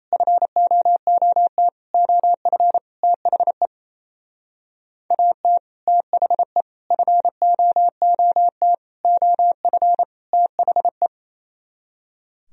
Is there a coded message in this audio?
Yes: Morse code from 0.1 to 11.1 s, heard as a keyed tone, about 36 wpm, 700 Hz. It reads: FOOT OF THE AT THE FOOT OF THE